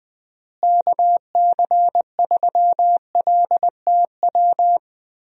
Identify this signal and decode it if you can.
Morse code, heard as a keyed tone, about 20 wpm, 700 Hz. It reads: KC3LTW
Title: USA Call Signs - 6 Characters - Repeated 3x - 20wpm